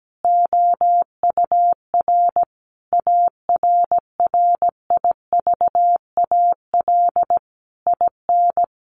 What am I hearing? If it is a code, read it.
Morse code, heard as a keyed tone, about 17 wpm, 700 Hz. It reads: OUR ARRIVAL IN
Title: A Sound Of Thunder 17wpm